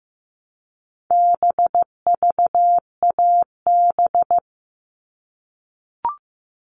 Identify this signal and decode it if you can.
Morse code, heard as a keyed tone, about 15 wpm, 700 Hz. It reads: BVAB
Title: All Cut Numbers - 4-Digit 15wpm